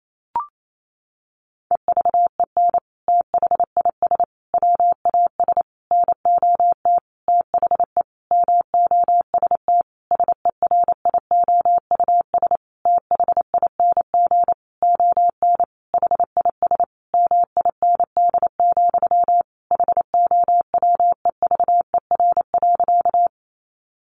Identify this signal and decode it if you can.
Morse code, heard as a keyed tone, about 28 wpm, 700 Hz. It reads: EVEN THIS WAS NOT THE MOST SERIOUS THING ON HIS MIND, HOWEVER.
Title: Sentences from Top 1100 Words 28wpm